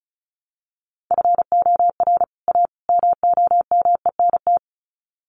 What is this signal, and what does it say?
Morse code, heard as a keyed tone, about 34 wpm, 700 Hz. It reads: FOR A MOMENT
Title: Sets of 3 Words 35wpm